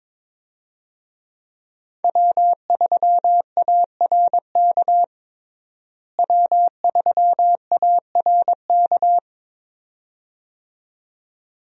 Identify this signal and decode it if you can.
Morse code, heard as a keyed tone, about 21 wpm, 700 Hz. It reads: W3ARK W3ARK
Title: USA Call Signs - 5 Characters - Repeated 2x - 22wpm